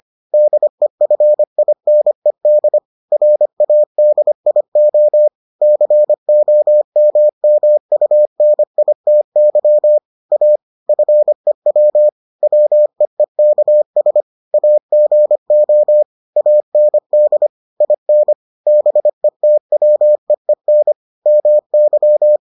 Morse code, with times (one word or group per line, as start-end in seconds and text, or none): 0.3-2.8 DEFINED
3.1-5.3 RADIO
5.6-10.0 COMMUNITY
10.3-10.6 A
10.9-12.1 FEW
12.4-14.2 WEEKS
14.5-16.0 AGO
16.4-17.5 AND
17.8-18.3 IN
18.7-20.9 BETWEEN
21.3-22.4 MY